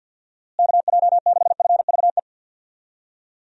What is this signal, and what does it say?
Morse code, heard as a keyed tone, about 52 wpm, 700 Hz. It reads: KJ6FVE
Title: USA Call Signs - 6 Characters - Repeated 3x - 50wpm